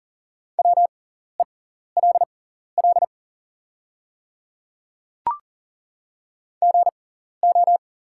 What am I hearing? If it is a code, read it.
Morse code, heard as a keyed tone, about 40 wpm, spaced as for 14 wpm, 700 Hz. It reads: WELL GO
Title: 6x Character Spacing - Top 200 Words - 40wpm